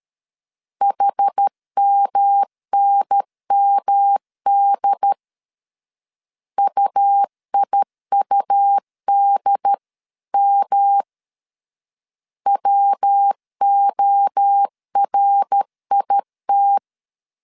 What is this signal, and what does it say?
Morse code, heard as a keyed tone, about 13 wpm, 800 Hz. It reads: HMNMD UIUDM WORIT
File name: CW / DET / 1 3 1500C S B 12wpm 2828